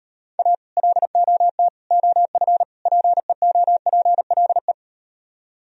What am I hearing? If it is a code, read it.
Morse code, heard as a keyed tone, about 38 wpm, 700 Hz. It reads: A LOT OF PEOPLE